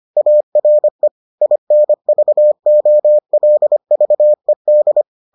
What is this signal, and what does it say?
Morse code, heard as a keyed tone, about 25 wpm, 600 Hz. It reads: ARE INVOLVED